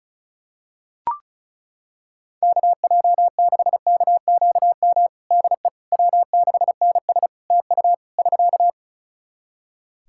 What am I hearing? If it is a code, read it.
Morse code, heard as a keyed tone, about 34 wpm, 700 Hz. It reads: KJ6KQM DE W6NS TU <SK>
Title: Practice QSO - Exchange 4 - Ending - 35wpm